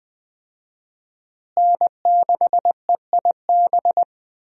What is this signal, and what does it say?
Morse code, heard as a keyed tone, about 20 wpm, 700 Hz. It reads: N6EIB
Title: USA Call Signs - 5 Characters - Repeated 3x - 20wpm